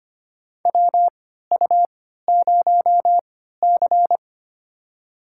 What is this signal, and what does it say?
Morse code, heard as a keyed tone, about 25 wpm, spaced as for 14 wpm, 700 Hz. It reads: WU0C
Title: USA Call Signs - 3x Character Spacing - 25wpm